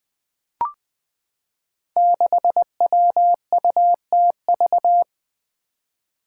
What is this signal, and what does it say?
Morse code, heard as a keyed tone, about 20 wpm, 700 Hz. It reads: 6WUTV
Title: Five Challenging Letter-Number Combinations 20wpm